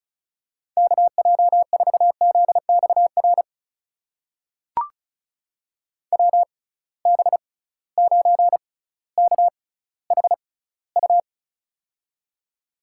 Morse code, heard as a keyed tone, about 36 wpm, spaced as for 18 wpm, 700 Hz. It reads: KJ4ZXR WB9KHU
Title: USA Call Signs - 6 Characters - 6x Character Spacing - 35wpm